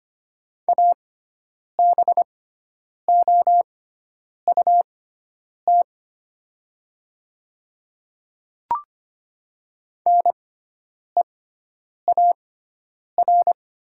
Morse code, heard as a keyed tone, about 25 wpm, spaced as for 9 wpm, 700 Hz. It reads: ABOUT NEAR
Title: 6x Character Spacing - Top 200 Words - 25wpm